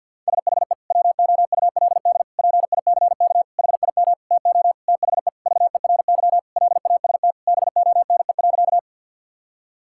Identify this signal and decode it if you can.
Morse code, heard as a keyed tone, about 52 wpm, 700 Hz. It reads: SHE WOULD PICK HIM TO THE VERY LAST BONE.
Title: Sentences from Top 1600 Words 50wpm